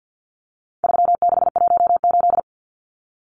Morse code, heard as a keyed tone, about 50 wpm, 700 Hz. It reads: 3618
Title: Numbers - 4-Digit 50wpm